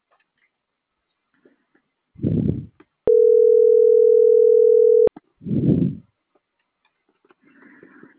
Ringback tone (North America), 440 + 480 Hz.